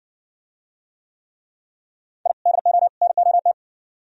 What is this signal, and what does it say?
Morse code, heard as a keyed tone, about 57 wpm, 700 Hz. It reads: I DO NOT